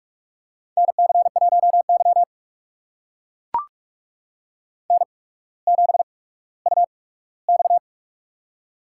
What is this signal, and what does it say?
Morse code, heard as a keyed tone, about 44 wpm, spaced as for 19 wpm, 690 Hz. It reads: NC1Y N7UX
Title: USA Call Signs - 4 Characters - 8x Character Spacing - 45wpm